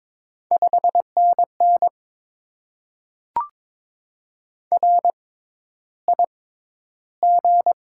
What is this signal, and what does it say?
Morse code, heard as a keyed tone, about 22 wpm, spaced as for 11 wpm, 700 Hz. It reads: 5NN RIG
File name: Top 20 QSO Elements - 6x Character Spacing - 22wpm